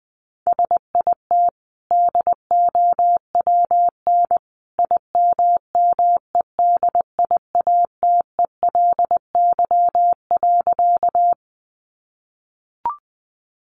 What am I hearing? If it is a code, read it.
Morse code, heard as a keyed tone, about 20 wpm, 700 Hz. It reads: SIT DOWN IMMEDIATELY.
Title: Sentences from Top 1100 Words 20wpm